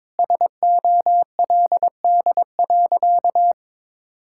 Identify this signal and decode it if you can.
Morse code, heard as a keyed tone, about 22 wpm, 700 Hz. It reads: SOLD.
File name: practice / 22wpm / Sentences with 4-Word Phrases - 22wpm